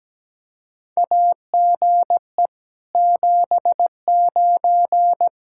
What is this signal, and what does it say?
Morse code, heard as a keyed tone, about 17 wpm, 700 Hz. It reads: AGE 79
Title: K1USN SST QSO Practice 17wpm